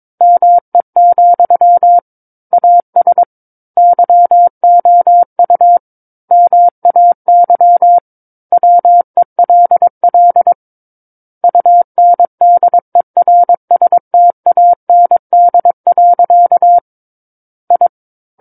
Morse code, heard as a keyed tone, about 22 wpm, 700 Hz. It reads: ME, AS YOU MAY WELL UNDERSTAND. I
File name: Book One / 13-Chapter 13 - 22 WPM